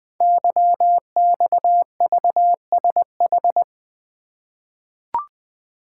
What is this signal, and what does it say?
Morse code, heard as a keyed tone, about 20 wpm, 700 Hz. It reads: YXVSH
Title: Code Groups - 5 Letter:Numbers - 20wpm